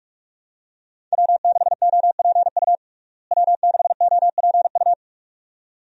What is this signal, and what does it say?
Morse code, heard as a keyed tone, about 46 wpm, 700 Hz. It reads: W6OPU W6OPU
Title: USA Call Signs - 5 Characters - Repeated 2x - 45wpm